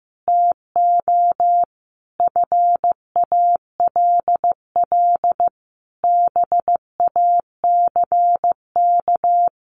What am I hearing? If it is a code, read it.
Morse code, heard as a keyed tone, about 15 wpm, 700 Hz. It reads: TO FALL BACK
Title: Sentences from Top 1100 Words 15wpm